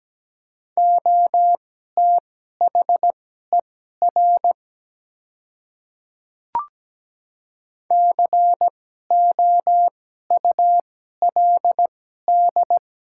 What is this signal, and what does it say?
Morse code, heard as a keyed tone, about 17 wpm, spaced as for 12 wpm, 700 Hz. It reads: OTHER COULD
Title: 2x Character Spacing - Top 100 Words - 17wpm